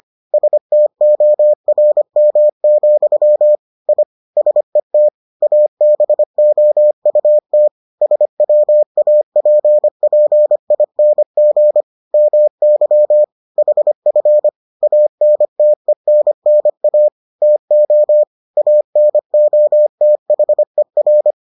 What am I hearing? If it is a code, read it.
Morse code, heard as a keyed tone, about 25 wpm, 600 Hz. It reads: STORM, I SET ABOUT SWAPPING MY HF ANTENNA TO ANOTHER